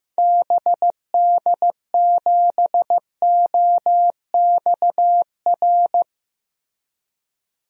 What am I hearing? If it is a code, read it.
Morse code, heard as a keyed tone, about 15 wpm, 700 Hz. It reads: BD7OXR
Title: DXCC-100 Call Signs - 6 Characters - Repeated 2x - 15wpm